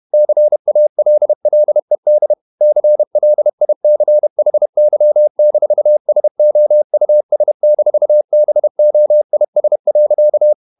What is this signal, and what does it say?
Morse code, heard as a keyed tone, about 31 wpm, 600 Hz. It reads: CALLED CLICHY-SOUS-BOIS.
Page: Never 31wpm | CW med Gnister